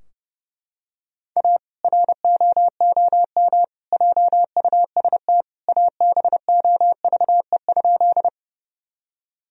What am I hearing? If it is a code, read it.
Morse code, heard as a keyed tone, about 31 wpm, 700 Hz. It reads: A ROOM JUST ABOVE?
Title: Sentences from Top 1000 Words - Review Part 2 - 30wpm